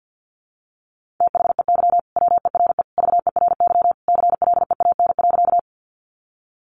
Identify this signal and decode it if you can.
Morse code, heard as a keyed tone, about 50 wpm, 700 Hz. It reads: THEY WERE VERY CLEAN.
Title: Sentences from Top 1100 Words 50wpm